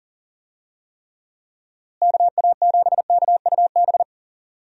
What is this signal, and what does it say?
Morse code, heard as a keyed tone, about 39 wpm, 700 Hz. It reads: KA7KUB
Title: USA Call Signs - 6 Characters - Repeated 2x - 40wpm